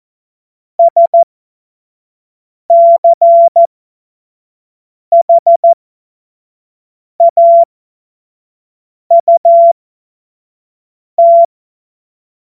Morse code, heard as a keyed tone, about 14 wpm, spaced as for 5 wpm, 680 Hz. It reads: SCHAUT
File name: Morsekurs / Witze / Witz 19 Dezember 2023 25 F70